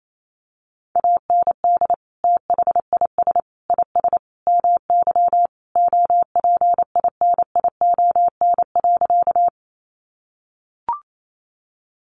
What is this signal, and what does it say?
Morse code, heard as a keyed tone, about 28 wpm, 700 Hz. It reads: AND THIS IS MY OPINION.
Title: Sentences from Top 1200 Words 28wpm